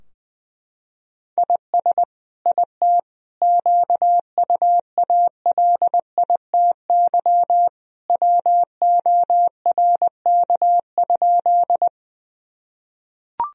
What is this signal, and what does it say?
Morse code, heard as a keyed tone, about 21 wpm, 700 Hz. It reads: IS IT QUALITY WORK?
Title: Sentences from Top 1000 Words - Review Part 4 - 20wpm